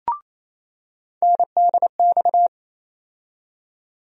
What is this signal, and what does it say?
Morse code, heard as a keyed tone, about 28 wpm, 700 Hz. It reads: NDX
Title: USA Call Sign Postfixes - 3 Letters 28wpm